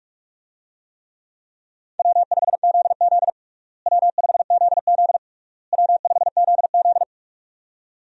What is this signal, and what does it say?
Morse code, heard as a keyed tone, about 46 wpm, 700 Hz. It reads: W5ZZ W5ZZ W5ZZ
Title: USA Call Signs - 4 Characters - Repeated 3x - 45wpm